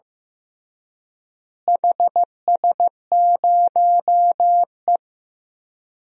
Morse code, heard as a keyed tone, about 15 wpm, 700 Hz. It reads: HS0E
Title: DXCC-100 Call Signs - 4 Characters - Repeated 3x - 15wpm